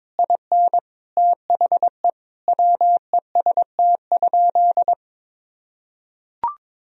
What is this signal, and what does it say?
Morse code, heard as a keyed tone, about 21 wpm, 700 Hz. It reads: IN THE WEST?
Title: Sentences from Top 1000 Words - Review Part 2 - 22wpm